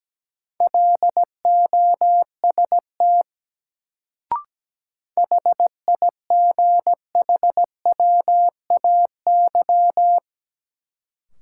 Morse code, heard as a keyed tone, about 17 wpm, 700 Hz. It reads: LOST HIGHWAY
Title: Sentences from Top 1400 Words - New Word Drill Alt - 17wpm